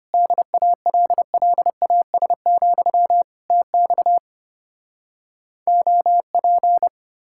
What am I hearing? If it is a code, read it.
Morse code, heard as a keyed tone, about 33 wpm, 700 Hz. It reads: DALLAS, TX OP